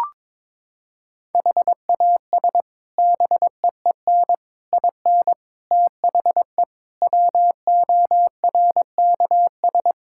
Morse code, heard as a keyed tone, about 22 wpm, 700 Hz. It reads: HAS BEEN IN THE WORKS
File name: practice / 22wpm / Sets of 6 Words - 22wpm